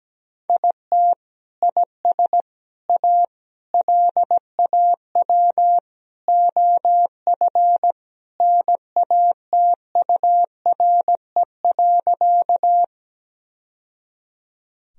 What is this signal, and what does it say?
Morse code, heard as a keyed tone, about 17 wpm, 700 Hz. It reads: IT IS A LAW OF NATURE.
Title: Sentences from Top 1000 Words - Review Part 3 - 17wpm